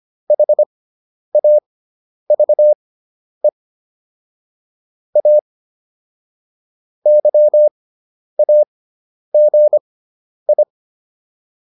Morse code, heard as a keyed tone, about 25 wpm, spaced as for 10 wpm, 600 Hz. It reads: HAVE A YAGI